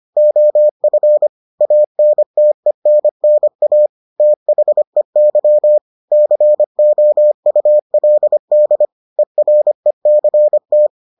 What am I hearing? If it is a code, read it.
Morse code, heard as a keyed tone, about 25 wpm, 600 Hz. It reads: OF ANTENNA THEY COULD ERECT